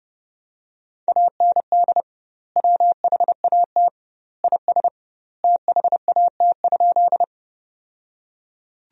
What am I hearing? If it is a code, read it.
Morse code, heard as a keyed tone, about 32 wpm, 700 Hz. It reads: AND WHAT IS THAT?
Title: 2x Word-Spacing - Sentences from Top 100 Words - 30wpm